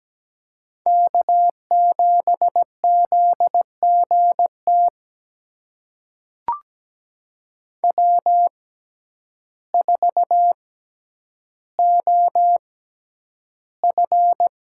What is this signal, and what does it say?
Morse code, heard as a keyed tone, about 17 wpm, spaced as for 9 wpm, 700 Hz. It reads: K7ZGT W4OF
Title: USA Call Signs - 5 Characters - 6x Character Spacing - 17wpm